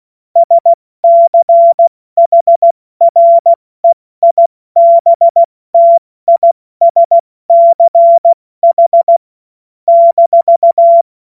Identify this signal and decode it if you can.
Morse code, heard as a keyed tone, about 16 wpm, 680 Hz. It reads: SCHREIBTISCH -
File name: Witz-02-September-2025-70-F80